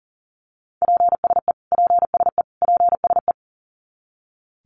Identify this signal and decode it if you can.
Morse code, heard as a keyed tone, about 40 wpm, 700 Hz. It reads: PSE PSE PSE
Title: Abbreviations Repeated 3x - 40wpm